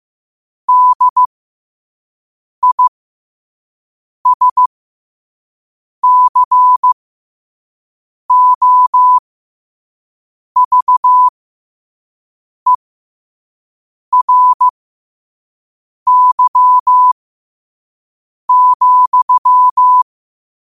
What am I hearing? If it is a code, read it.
Morse code, heard as a keyed tone, about 15 wpm, spaced as for 5 wpm, 1000 Hz. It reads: DISCOVERY,
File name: qotdfiles / 5wpm